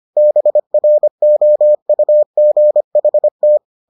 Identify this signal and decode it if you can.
Morse code, heard as a keyed tone, about 25 wpm, 600 Hz. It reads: BROUGHT